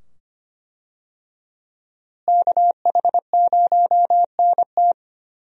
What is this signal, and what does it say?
Morse code, heard as a keyed tone, about 25 wpm, 700 Hz. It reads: KH0NT
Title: USA Call Signs - 5 Characters - Repeated 2x - 25wpm